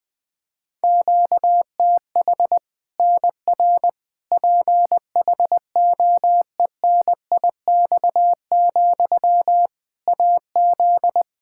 Morse code, heard as a keyed tone, about 21 wpm, 700 Hz. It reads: QTH NR PHOENIX, AZ
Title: K1USN SST QSO Practice 20wpm